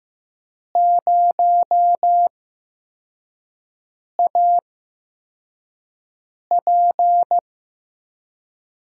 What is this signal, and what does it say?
Morse code, heard as a keyed tone, about 15 wpm, spaced as for 4 wpm, 700 Hz. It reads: 0AP
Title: USA Call Signs - 5 Characters - 8x Character Spacing - 15wpm